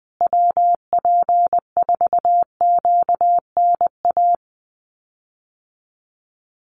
Morse code, heard as a keyed tone, about 20 wpm, 700 Hz. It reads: WP4QNA
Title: USA Call Signs Repeated 3x 20wpm